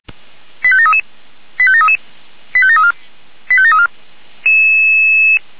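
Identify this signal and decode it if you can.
Five-tone selective call (ZVEI), heard as quick runs of five tones, tones of about 70 ms. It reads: 85733 85733 85734 85734
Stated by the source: ZVEI five-tone selective call 89291 received by SDR.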